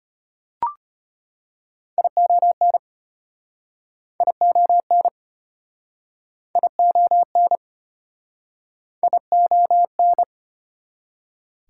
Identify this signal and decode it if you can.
Morse code, heard as a keyed tone, about 34 wpm, 700 Hz. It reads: ION ION ION ION